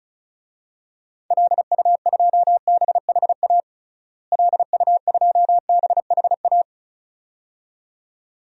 Morse code, heard as a keyed tone, about 36 wpm, 700 Hz. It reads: LU2BHA LU2BHA
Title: DXCC-100 Call Signs - 6 Characters - Repeated 2x - 35wpm